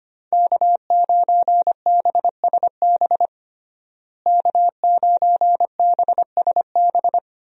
Morse code, heard as a keyed tone, about 25 wpm, 700 Hz. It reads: K9BSB K9BSB
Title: USA Call Signs - 5 Characters - Repeated 3x - 25wpm